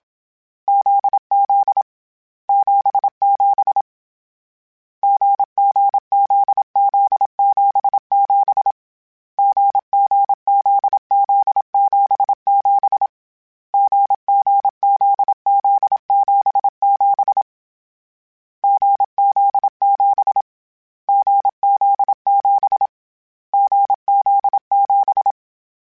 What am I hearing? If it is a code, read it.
Morse code, heard as a keyed tone, about 27 wpm, 800 Hz. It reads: ZZ 77 GGZZ77 GGZZ77 GGZZ77 GZ7 GZ7 GZ7
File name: Difficult Combos / Difficult Combos-mp3s / GZ GZ7 @25